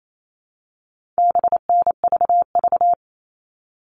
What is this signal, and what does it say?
Morse code, heard as a keyed tone, about 28 wpm, 700 Hz. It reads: BNVV
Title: All Cut Numbers - 4-Digit 28wpm